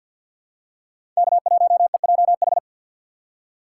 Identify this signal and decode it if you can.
Morse code, heard as a keyed tone, about 53 wpm, 700 Hz. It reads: K1EJH
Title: USA Call Signs - 5 Characters - 3x Character Spacing - 50wpm